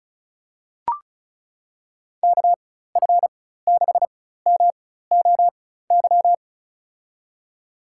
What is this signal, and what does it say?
Morse code, heard as a keyed tone, about 36 wpm, spaced as for 16 wpm, 700 Hz. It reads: KF6MOY
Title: USA Call Signs - 6 Characters - 4x Character Spacing - 35wpm